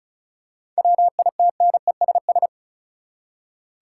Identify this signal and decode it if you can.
Morse code, heard as a keyed tone, about 36 wpm, 700 Hz. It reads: WITNESS
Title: Seven Letter Words - 35wpm